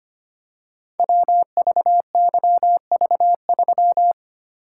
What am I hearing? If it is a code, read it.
Morse code, heard as a keyed tone, about 25 wpm, 700 Hz. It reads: WVYV3
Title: Code Groups - 5 Letter:Numbers - 25wpm